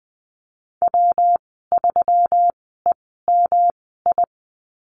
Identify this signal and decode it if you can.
Morse code, heard as a keyed tone, about 20 wpm, spaced as for 15 wpm, 700 Hz. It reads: W3EMI
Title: 128 Lesson 21g - Number 6 - Call Signs 20wpm